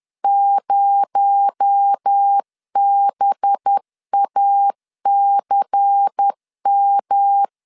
Morse code, heard as a keyed tone, about 10 wpm, 800 Hz. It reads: 0BACM
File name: LEZIONI / 9 2867 10wpm 2733